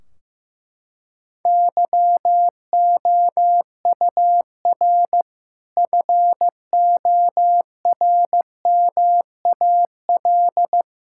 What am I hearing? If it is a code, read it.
Morse code, heard as a keyed tone, about 15 wpm, 700 Hz. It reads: YOUR FORMAL
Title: Sentences from Top 1600 Words 15wpm